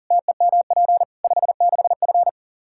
Morse code, heard as a keyed tone, about 37 wpm, 700 Hz. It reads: TEMP 56F
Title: Practice QSO - Exchange 2 - Info - 40wpm